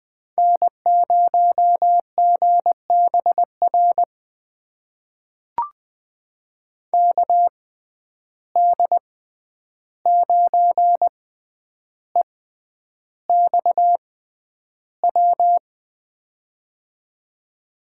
Morse code, heard as a keyed tone, about 20 wpm, spaced as for 10 wpm, 700 Hz. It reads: N0GBR KD9EXW
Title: USA Call Signs - 6x Character Spacing - 20wpm